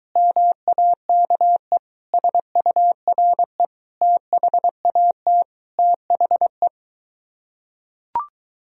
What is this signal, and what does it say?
Morse code, heard as a keyed tone, about 23 wpm, 700 Hz. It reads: MAKE SURE THAT THE